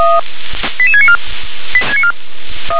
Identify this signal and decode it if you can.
Five-tone selective call (ZVEI), heard as quick runs of five tones, tones of about 70 ms. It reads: 88694
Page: /357/SDR/ZVEI/